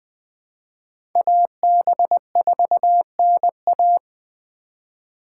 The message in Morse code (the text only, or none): AB4NA